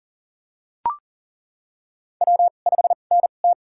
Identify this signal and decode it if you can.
Morse code, heard as a keyed tone, about 40 wpm, spaced as for 29 wpm, 700 Hz. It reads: W5NT
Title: USA Call Signs - 4 Characters - 2x Character Spacing - 40wpm